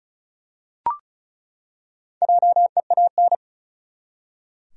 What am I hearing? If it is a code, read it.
Morse code, heard as a keyed tone, about 35 wpm, 700 Hz. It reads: JEAN
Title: USA Common Names - Max 4 Letters - 35wpm